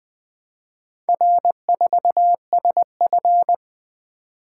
Morse code, heard as a keyed tone, about 20 wpm, 700 Hz. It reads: R4SF
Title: Four Letter-Number Combinations - 20wpm